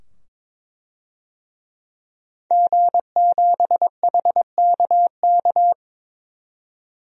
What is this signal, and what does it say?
Morse code, heard as a keyed tone, about 22 wpm, 700 Hz. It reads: G7HKK
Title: DXCC-100 Call Signs - 5 Characters - 22wpm